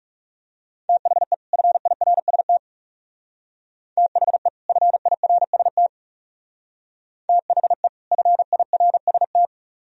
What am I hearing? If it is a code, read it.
Morse code, heard as a keyed tone, about 37 wpm, 700 Hz. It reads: THE FIRST THE FIRST THE FIRST
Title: Speed-Racing - Sets of 2 Words 30wpm